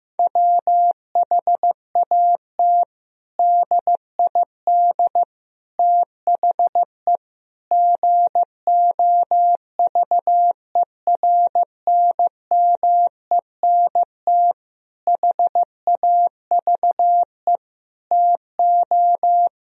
Morse code, heard as a keyed tone, about 14 wpm, 700 Hz. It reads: WHAT DID THE GOVERNMENT HAVE TO
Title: Sentences from Top 1000 Words - Review Part 3 - 15wpm